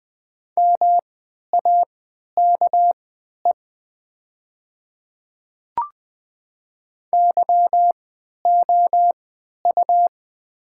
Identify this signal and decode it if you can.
Morse code, heard as a keyed tone, about 20 wpm, spaced as for 11 wpm, 700 Hz. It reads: MAKE YOU
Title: 3x Character Spacing - Top 100 Words - 20wpm